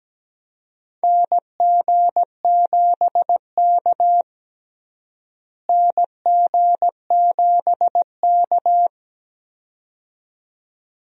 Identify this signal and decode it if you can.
Morse code, heard as a keyed tone, about 17 wpm, 700 Hz. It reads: NG7K NG7K
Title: USA Call Signs - 4 Characters - Repeated 2x - 17wpm